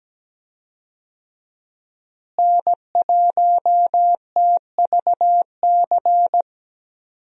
Morse code, heard as a keyed tone, about 17 wpm, 700 Hz. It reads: N1TVC
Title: USA Call Signs - 5 Characters - Repeated 2x - 17wpm